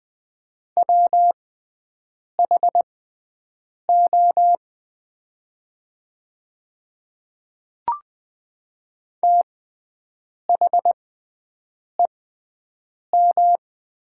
Morse code, heard as a keyed tone, about 20 wpm, spaced as for 7 wpm, 700 Hz. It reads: WHO THEM
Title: 6x Character Spacing - Top 200 Words - 20wpm